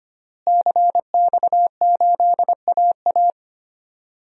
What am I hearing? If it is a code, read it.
Morse code, heard as a keyed tone, about 25 wpm, 700 Hz. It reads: CX8AA
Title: DXCC-100 Call Signs - 5 Characters - Repeated 3x - 25wpm